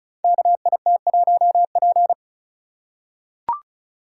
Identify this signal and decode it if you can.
Morse code, heard as a keyed tone, about 36 wpm, 700 Hz. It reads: KIT1P
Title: Code Groups - 5 Letter:Numbers - 35wpm